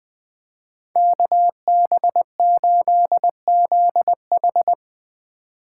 Morse code, heard as a keyed tone, about 21 wpm, 700 Hz. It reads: KB8ZH